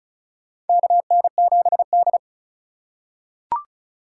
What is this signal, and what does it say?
Morse code, heard as a keyed tone, about 34 wpm, 700 Hz. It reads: KN7D
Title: K1USN SST QSO Practice - 35wpm